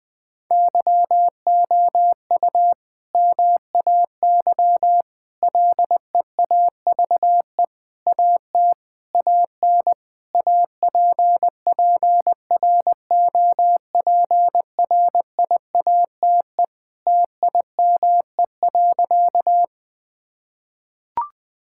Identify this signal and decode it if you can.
Morse code, heard as a keyed tone, about 20 wpm, 700 Hz. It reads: YOU MAY LEAVE AT AN APPROPRIATE TIME.
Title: Sentences from Top 1000 Words - Review Part 5 - 20wpm